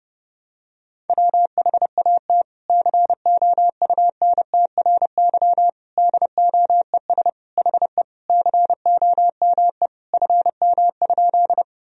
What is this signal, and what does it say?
Morse code, heard as a keyed tone, about 30 wpm, 700 Hz. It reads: WHAT COUNTRY DOES HE COME FM?
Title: Sentences from Top 300 Words and Abbreviations 30wpm